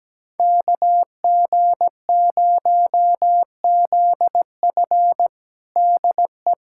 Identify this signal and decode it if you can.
Morse code, heard as a keyed tone, about 17 wpm, 700 Hz. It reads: KG0ZF DE
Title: USA Callsign DE Callsign - 5 Characters - 17wpm